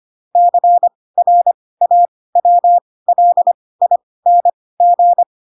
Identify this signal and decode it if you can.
Morse code, heard as a keyed tone, about 25 wpm, spaced as for 18 wpm, 700 Hz. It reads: CRAWLING
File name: Pinocchio / Pinocchio ch4 12 25 ik1hge